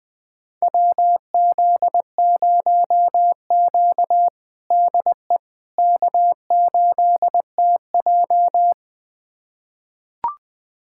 Morse code, heard as a keyed tone, about 21 wpm, 700 Hz. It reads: WZ0Q DE K8TJ
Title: USA Callsign DE Callsign - 4 Characters - 20wpm